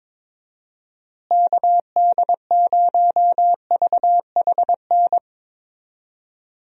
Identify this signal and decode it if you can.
Morse code, heard as a keyed tone, about 22 wpm, 700 Hz. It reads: KD0VHN
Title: USA Call Signs - 6 Characters - Repeated 2x - 22wpm